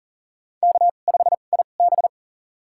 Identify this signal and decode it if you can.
Morse code, heard as a keyed tone, about 37 wpm, spaced as for 28 wpm, 700 Hz. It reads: K5IB
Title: USA Call Signs - 4 Characters - 2x Character Spacing - 40wpm